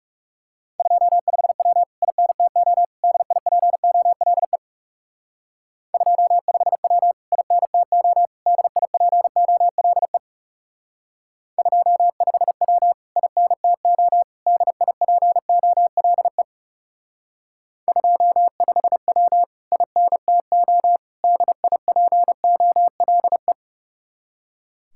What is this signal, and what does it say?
Morse code, heard as a keyed tone, about 37 wpm, 700 Hz. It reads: 25W INTO DIPOLE 25W INTO DIPOLE 25W INTO DIPOLE 25W INTO DIPOLE
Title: Speed-Racing Common QSO Phrases Encore - 30wpm